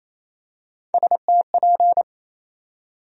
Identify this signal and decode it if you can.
Morse code, heard as a keyed tone, about 28 wpm, 700 Hz. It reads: STP
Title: USA Call Sign Postfixes - 3 Letters 28wpm